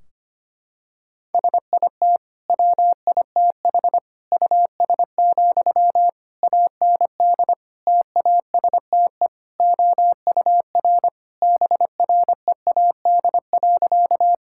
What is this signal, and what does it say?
Morse code, heard as a keyed tone, about 25 wpm, 700 Hz. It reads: SIT WITH US, AND TASTE OUR BREAD.
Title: Sentences from Top 1600 Words 25wpm